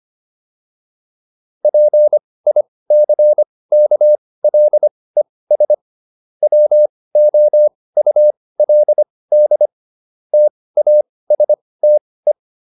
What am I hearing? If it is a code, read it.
Morse code, heard as a keyed tone, about 25 wpm, spaced as for 18 wpm, 600 Hz. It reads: PICKLES WOULD TASTE